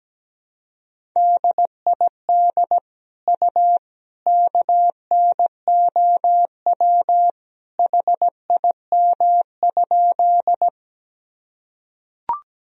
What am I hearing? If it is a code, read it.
Morse code, heard as a keyed tone, about 17 wpm, 700 Hz. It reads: DID U KNOW HIM?
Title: Sentences from Top 100 Words and Abbreviations 17wpm